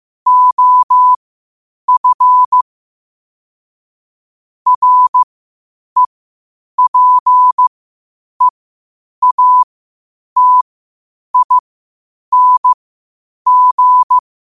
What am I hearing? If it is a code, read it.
Morse code, heard as a keyed tone, about 15 wpm, spaced as for 8 wpm, 1000 Hz. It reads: OF REPEATING
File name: qotdfiles / 8wpm